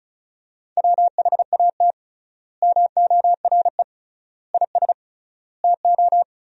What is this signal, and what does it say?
Morse code, heard as a keyed tone, about 34 wpm, 700 Hz. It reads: WHAT MORE IS TO
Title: 3x Word-Spacing - Sentences from Top 500 Words - 35wpm